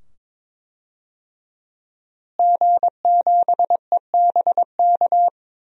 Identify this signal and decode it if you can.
Morse code, heard as a keyed tone, about 22 wpm, 700 Hz. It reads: G7EBK
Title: DXCC-100 Call Signs - 5 Characters - Repeated 2x - 22wpm